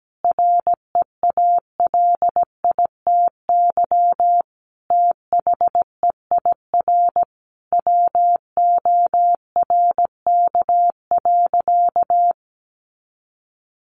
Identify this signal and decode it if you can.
Morse code, heard as a keyed tone, about 17 wpm, 700 Hz. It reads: REALITY THEIR WORK.
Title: Sentences from Top 1400 Words 17wpm